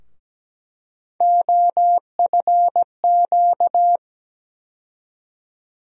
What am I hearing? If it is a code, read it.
Morse code, heard as a keyed tone, about 17 wpm, 700 Hz. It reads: OFQ
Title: USA Call Sign Postfixes - 3 Letters 17wpm